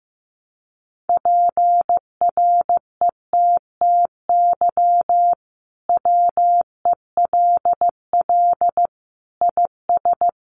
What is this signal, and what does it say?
Morse code, heard as a keyed tone, about 15 wpm, 700 Hz. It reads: PRETTY WELL IS
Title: Sentences from Top 1000 Words 15wpm